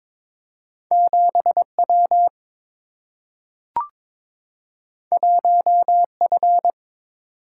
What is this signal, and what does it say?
Morse code, heard as a keyed tone, about 22 wpm, 700 Hz. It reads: 7 W 1 F
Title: Single-Letter Single-Number 22wpm